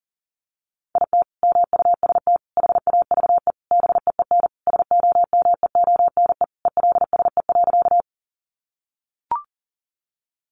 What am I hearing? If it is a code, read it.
Morse code, heard as a keyed tone, about 40 wpm, 700 Hz. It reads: IT MUST HAVE BEEN SOMEONE ELSE.
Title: Sentences from Top 800 Words 40wpm